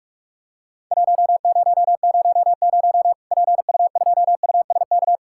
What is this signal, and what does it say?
Morse code, heard as a keyed tone, about 47 wpm, 700 Hz. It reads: 1000 PU2USK
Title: Worked All Europe DX Contest - QTC Exchange - 45wpm